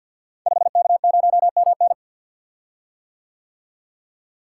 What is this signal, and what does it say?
Morse code, heard as a keyed tone, about 52 wpm, 700 Hz. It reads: 5K0MN
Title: Code Groups - 5 Letter:Numbers - 50wpm